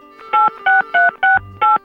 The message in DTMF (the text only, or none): 86368